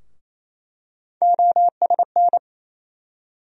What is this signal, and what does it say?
Morse code, heard as a keyed tone, about 28 wpm, 700 Hz. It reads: OSN
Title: USA Call Sign Postfixes - 3 Letters 28wpm